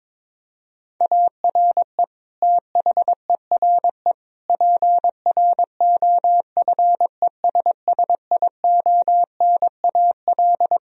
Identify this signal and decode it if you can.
Morse code, heard as a keyed tone, about 22 wpm, 700 Hz. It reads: ARE THERE PROFESSIONAL
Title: Sentences from Top 1000 Words - Review Part 4 - 22wpm